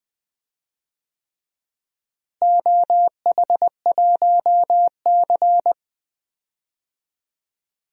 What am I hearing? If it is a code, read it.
Morse code, heard as a keyed tone, about 20 wpm, 700 Hz. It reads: OH1C
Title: DXCC-100 Call Signs - 4 Characters - Repeated 3x - 20wpm